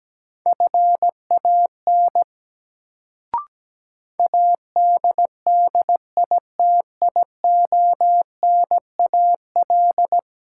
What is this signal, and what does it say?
Morse code, heard as a keyed tone, about 17 wpm, 700 Hz. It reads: FAN ADDITIONAL
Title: Sentences from Top 1000 Words - Review Part 4 - New Word Drill - 17wpm